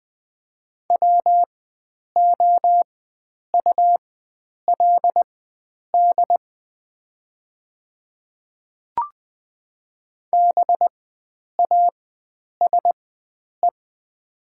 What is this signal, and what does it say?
Morse code, heard as a keyed tone, about 20 wpm, spaced as for 9 wpm, 700 Hz. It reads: WOULD BASE
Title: 4x Character Spacing - Top 500 Words - 20wpm